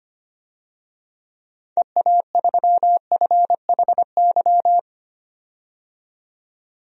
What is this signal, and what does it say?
Morse code, heard as a keyed tone, about 25 wpm, 700 Hz. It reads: EA3FHY
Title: DXCC-100 Call Signs - 6 Characters - 25wpm